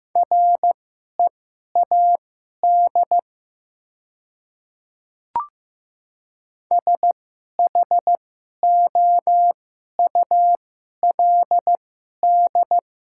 Morse code, heard as a keyed tone, about 15 wpm, spaced as for 11 wpm, 700 Hz. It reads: READ SHOULD
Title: 2x Character Spacing - Top 300 Words - 15wpm